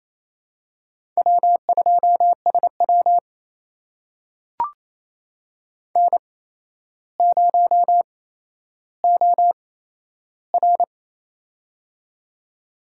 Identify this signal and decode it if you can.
Morse code, heard as a keyed tone, about 28 wpm, spaced as for 12 wpm, 700 Hz. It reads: W2SW N0OR
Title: USA Call Signs - 4 Characters - 8x Character Spacing - 28wpm